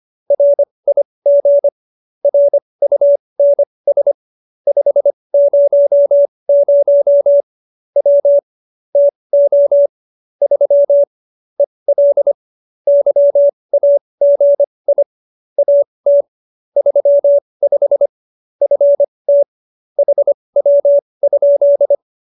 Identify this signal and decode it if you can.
Morse code, heard as a keyed tone, about 25 wpm, spaced as for 20 wpm, 570 Hz. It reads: RIG RUNS 500 W TO 3 EL YAGI AT 35 FT HW?